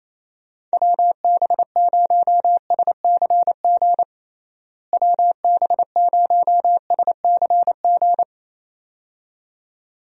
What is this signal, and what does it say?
Morse code, heard as a keyed tone, about 28 wpm, 700 Hz. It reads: WB0SCG WB0SCG
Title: USA Call Signs - 6 Characters - Repeated 2x - 28wpm